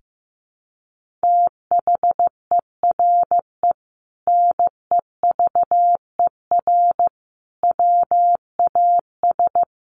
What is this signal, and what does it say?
Morse code, heard as a keyed tone, about 15 wpm, 700 Hz. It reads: THERE NEVER WAS
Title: Sentences from Top 1500 Words 15wpm